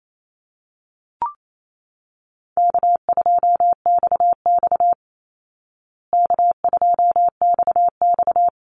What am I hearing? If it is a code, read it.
Morse code, heard as a keyed tone, about 28 wpm, 700 Hz. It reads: K2XX K2XX
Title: Call Signs Repeated 3x - 28wpm